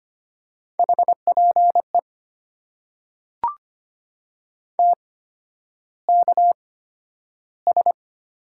Morse code, heard as a keyed tone, about 25 wpm, spaced as for 11 wpm, 690 Hz. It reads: HPE TKS
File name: Top 90 QSO Elements - 8x Character Spacing - 25wpm